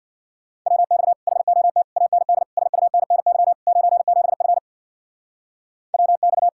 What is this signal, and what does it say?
Morse code, heard as a keyed tone, about 61 wpm, 700 Hz. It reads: WX HOT AND SUNNY 97F WX
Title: Speed-Racing Common QSO Phrases - 50wpm